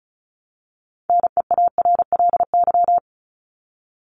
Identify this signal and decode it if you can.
Morse code, heard as a keyed tone, about 34 wpm, 700 Hz. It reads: NEARLY
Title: Top 1001-1250 Words 35wpm